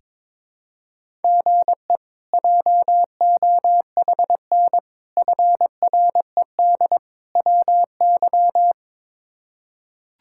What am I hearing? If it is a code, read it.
Morse code, heard as a keyed tone, about 21 wpm, 700 Hz. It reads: GE JOHN FRED WY
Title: K1USN SST QSO Practice - 22wpm